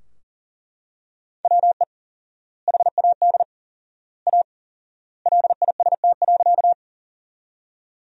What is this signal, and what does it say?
Morse code, heard as a keyed tone, about 37 wpm, 700 Hz. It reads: WE HAD A LIST.
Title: 4x Word-Spacing - Sentences from Top 400 Words - 40wpm